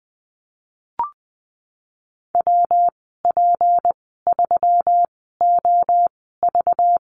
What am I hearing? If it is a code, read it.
Morse code, heard as a keyed tone, about 20 wpm, spaced as for 15 wpm, 700 Hz. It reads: WP3OV
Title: 193 Lesson 31g - Letter v - Call Signs 20wpm